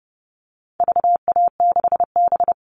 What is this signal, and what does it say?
Morse code, heard as a keyed tone, about 30 wpm, 700 Hz. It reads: VA6B
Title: All Cut Numbers - 4-Digit 30wpm